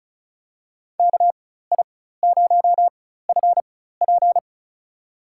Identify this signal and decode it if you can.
Morse code, heard as a keyed tone, about 34 wpm, spaced as for 16 wpm, 700 Hz. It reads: KI0FP
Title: USA Call Signs - 5 Characters - 4x Character Spacing - 35wpm